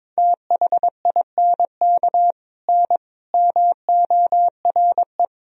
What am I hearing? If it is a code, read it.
Morse code, heard as a keyed tone, about 22 wpm, 700 Hz. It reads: THINK N MORE
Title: Sentences from Top 200 Words and Abbreviations 22wpm